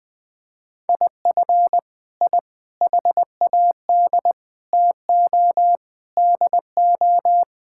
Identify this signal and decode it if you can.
Morse code, heard as a keyed tone, about 20 wpm, 700 Hz. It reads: IF I HAD TO DO
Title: Sentences with 9-Word Phrases - Spoken Beginning - 20wpm